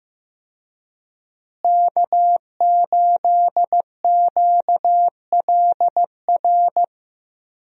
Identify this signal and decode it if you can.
Morse code, heard as a keyed tone, about 15 wpm, 700 Hz. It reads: K8QLR
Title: USA Call Signs - 5 Characters - Repeated 3x - 15wpm